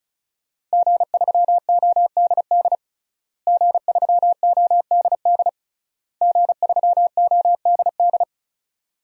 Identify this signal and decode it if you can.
Morse code, heard as a keyed tone, about 34 wpm, 700 Hz. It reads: G3ODD G3ODD G3ODD
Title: DXCC-100 Call Signs - 5 Characters - Repeated 3x - 35wpm